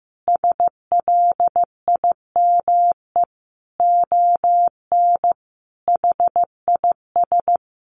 Morse code, heard as a keyed tone, about 15 wpm, 700 Hz. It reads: SLIME ON HIS
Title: A Sound Of Thunder 15wpm